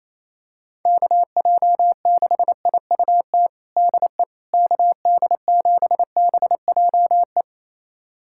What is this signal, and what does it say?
Morse code, heard as a keyed tone, about 27 wpm, 700 Hz. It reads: KJ6IUT DE KD7BJE
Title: USA Callsign DE Callsign 28wpm